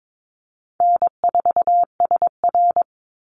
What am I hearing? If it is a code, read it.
Morse code, heard as a keyed tone, about 22 wpm, 690 Hz. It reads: N4SR
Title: Easy Call Signs Repeated 2x - 22wpm